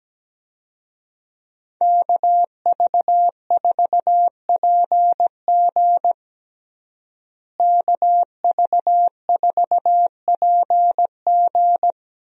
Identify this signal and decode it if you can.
Morse code, heard as a keyed tone, about 17 wpm, 700 Hz. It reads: KV4PG KV4PG
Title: USA Call Signs - 5 Characters - Repeated 3x - 17wpm